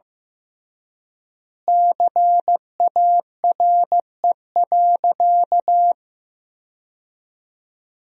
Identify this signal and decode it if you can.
Morse code, heard as a keyed tone, about 15 wpm, 700 Hz. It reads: CARE.
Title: 3x Word-Spacing - Sentences from Top 300 Words - 15wpm